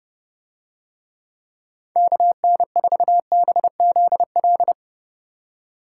Morse code, heard as a keyed tone, about 30 wpm, 700 Hz. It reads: KN4BZL